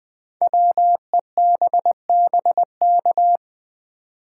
Morse code, heard as a keyed tone, about 20 wpm, 700 Hz. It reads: WEBBK
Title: Code Groups - 5 Letter:Numbers - 20wpm